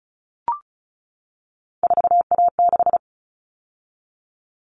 Morse code, heard as a keyed tone, about 34 wpm, 700 Hz. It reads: 4A6
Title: All Cut Numbers - 3-Digit 35wpm